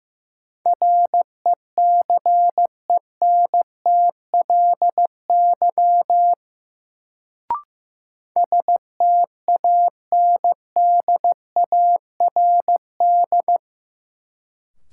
Morse code, heard as a keyed tone, about 15 wpm, 700 Hz. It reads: RECENTLY STANDARD
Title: Sentences from Top 1200 Words - New Word Drill Alt - 15wpm